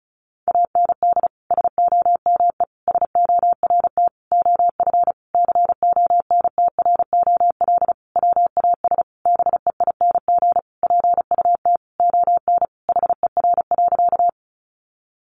Morse code, heard as a keyed tone, about 34 wpm, 700 Hz. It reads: AND SOME SORT OF CONTROL WAS BEING PUT ON HER.
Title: Sentences from Top 1300 Words 35wpm